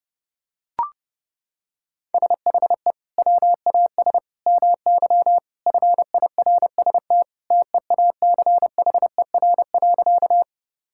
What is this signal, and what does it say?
Morse code, heard as a keyed tone, about 28 wpm, 700 Hz. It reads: SHE WAS MY FIRST TEACHER.
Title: Sentences from Top 1000 Words - Review Part 6 - 30wpm